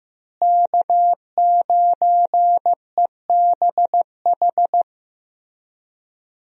Morse code, heard as a keyed tone, about 15 wpm, 700 Hz. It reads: K9EBH
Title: USA Call Signs - 5 Characters - Repeated 2x - 15wpm